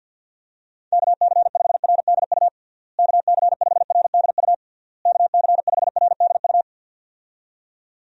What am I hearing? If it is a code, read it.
Morse code, heard as a keyed tone, about 52 wpm, 700 Hz. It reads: KC5RDU KC5RDU KC5RDU
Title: USA Call Signs - 6 Characters - Repeated 3x - 50wpm